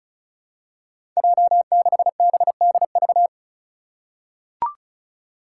Morse code, heard as a keyed tone, about 34 wpm, 700 Hz. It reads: J6BDV
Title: Five Challenging Letter-Number Combinations 35wpm